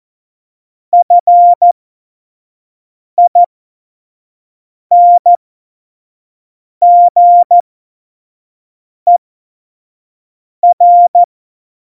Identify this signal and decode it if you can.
Morse code, heard as a keyed tone, about 14 wpm, spaced as for 5 wpm, 700 Hz. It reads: FINGER
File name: Morsekurs / Witze / Witz 27. Februar 2018, 25 F70